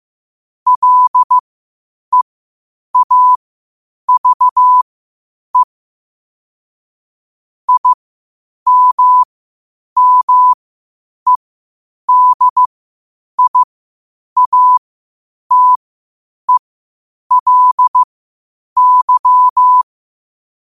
Morse code, heard as a keyed tone, about 15 wpm, spaced as for 8 wpm, 1000 Hz. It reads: LEAVE IMMEDIATELY
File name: qotdfiles / 8wpm